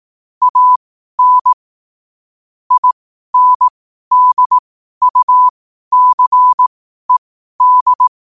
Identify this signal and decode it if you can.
Morse code, heard as a keyed tone, about 18 wpm, spaced as for 12 wpm, 1000 Hz. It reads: AN INDUCED
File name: qotdfiles / 12wpm